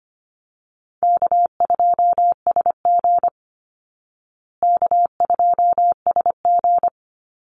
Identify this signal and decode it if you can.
Morse code, heard as a keyed tone, about 25 wpm, 700 Hz. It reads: K2SG K2SG
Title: Call Signs Repeated 3x - 25wpm